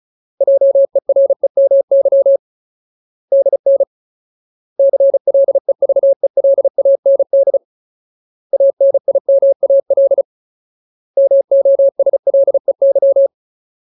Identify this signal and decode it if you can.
Morse code, heard as a keyed tone, about 34 wpm, 550 Hz. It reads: JEREMY DN CLEVELAND ANIMAL MOSLEY
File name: Ultimate Word List / 35WPM Ultimate